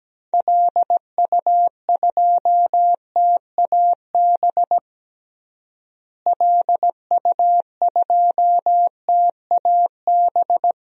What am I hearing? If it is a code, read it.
Morse code, heard as a keyed tone, about 17 wpm, 700 Hz. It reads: LU2TAB LU2TAB
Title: DXCC-100 Call Signs - 6 Characters - Repeated 2x - 17wpm